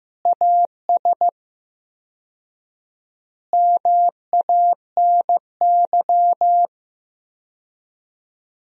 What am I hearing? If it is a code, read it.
Morse code, heard as a keyed tone, about 16 wpm, 700 Hz. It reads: AS MANY
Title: 4x Word-Spacing - Sentences from Top 400 Words - 15wpm